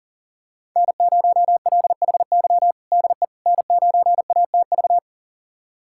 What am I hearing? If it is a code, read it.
Morse code, heard as a keyed tone, about 40 wpm, 700 Hz. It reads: N0LHY DE N9ATV
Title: USA Callsign DE Callsign - 5 Characters - 40wpm